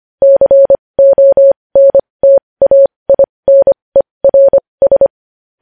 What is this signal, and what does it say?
Morse code, heard as a keyed tone, about 25 wpm, spaced as for 20 wpm, 550 Hz. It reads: CONTAINERS
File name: podcast / News headlines-1x-25x20